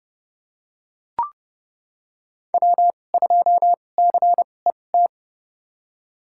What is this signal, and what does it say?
Morse code, heard as a keyed tone, about 31 wpm, spaced as for 22 wpm, 690 Hz. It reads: W2CET
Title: USA Call Signs - 5 Characters - 2x Character Spacing - 30wpm